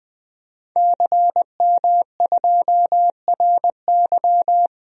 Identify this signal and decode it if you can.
Morse code, heard as a keyed tone, about 21 wpm, 700 Hz. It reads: CM2RY
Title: DXCC-100 Call Signs - 5 Characters - 20wpm